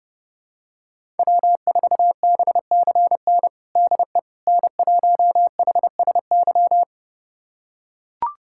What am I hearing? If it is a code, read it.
Morse code, heard as a keyed tone, about 30 wpm, 700 Hz. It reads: W4BCN DE N1HSY
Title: USA Callsign DE Callsign - 5 Characters - 30wpm